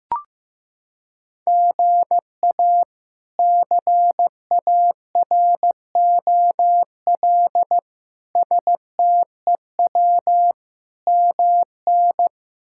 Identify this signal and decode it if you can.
Morse code, heard as a keyed tone, about 16 wpm, 700 Hz. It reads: GA CAROL STEW MN
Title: K1USN SST QSO Practice - 15wpm